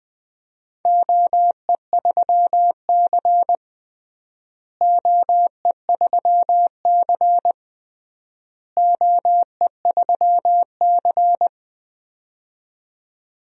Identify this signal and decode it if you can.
Morse code, heard as a keyed tone, about 21 wpm, 700 Hz. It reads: OE3C OE3C OE3C
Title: DXCC-100 Call Signs - 4 Characters - Repeated 3x - 20wpm